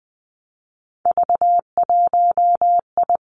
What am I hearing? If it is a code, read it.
Morse code, heard as a keyed tone, about 20 wpm, 700 Hz. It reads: V1I
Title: Three Challenging Letter-Number Combinations 20wpm